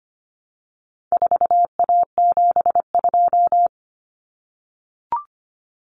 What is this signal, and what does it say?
Morse code, heard as a keyed tone, about 25 wpm, 700 Hz. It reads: 4A72
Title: Common Cut Numbers - 4-Digit 25wpm